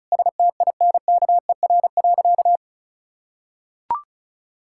Morse code, heard as a keyed tone, about 36 wpm, 700 Hz. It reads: STINKER.
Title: Pangram Challenge v2 - 35wpm